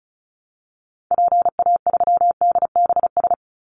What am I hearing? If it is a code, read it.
Morse code, heard as a keyed tone, about 36 wpm, 700 Hz. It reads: PA3DBS
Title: Call Signs - 35wpm